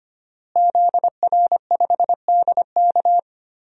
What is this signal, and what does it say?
Morse code, heard as a keyed tone, about 25 wpm, 700 Hz. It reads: ZR5DK
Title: DXCC-100 Call Signs - 5 Characters - Repeated 2x - 25wpm